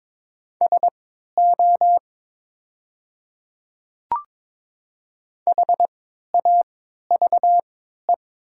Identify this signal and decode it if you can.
Morse code, heard as a keyed tone, about 22 wpm, spaced as for 12 wpm, 700 Hz. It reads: SO HAVE
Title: 3x Character Spacing - Top 100 Words - 22wpm